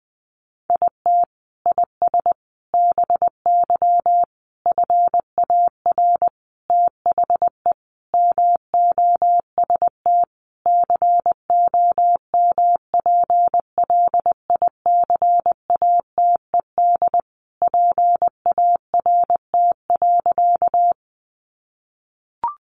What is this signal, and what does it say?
Morse code, heard as a keyed tone, about 20 wpm, 700 Hz. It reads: IT IS BY FAR THE MOST COMPLICATED PART.
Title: Sentences from Top 1200 Words 20wpm